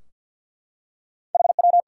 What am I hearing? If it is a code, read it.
Morse code, heard as a keyed tone, about 53 wpm, 700 Hz. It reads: HW
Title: Practice QSO - Exchange 3 - More Info - 50wpm